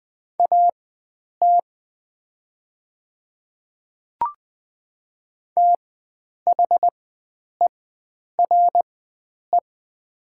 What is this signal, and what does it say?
Morse code, heard as a keyed tone, about 20 wpm, spaced as for 9 wpm, 700 Hz. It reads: AT THERE
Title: 4x Character Spacing - Top 100 Words - 20wpm